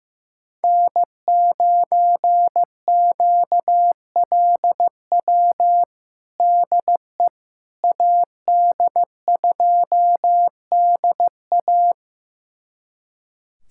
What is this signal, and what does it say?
Morse code, heard as a keyed tone, about 15 wpm, 700 Hz. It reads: N9QLW DE AD2DA
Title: USA Callsign DE Callsign - 5 Characters - 15wpm